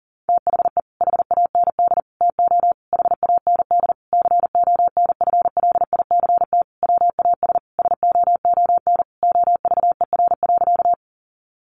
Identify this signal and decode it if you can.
Morse code, heard as a keyed tone, about 40 wpm, 700 Hz. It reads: THE HAND TO HAND CONFLICT WAS SOON OVER.
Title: Sentences from Top 1300 Words 40wpm